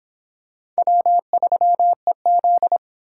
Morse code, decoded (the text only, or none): W3EZ